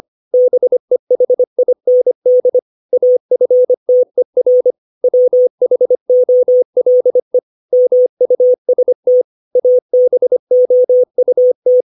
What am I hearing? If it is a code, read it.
Morse code, heard as a keyed tone, about 25 wpm, 500 Hz. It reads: BEHIND AFTER WHOLE MUST ABOUT
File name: 632 words / 632 Words 25WPM